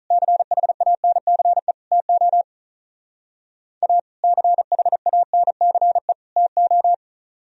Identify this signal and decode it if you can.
Morse code, heard as a keyed tone, about 44 wpm, 700 Hz. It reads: CHANCE TO A CHANCE TO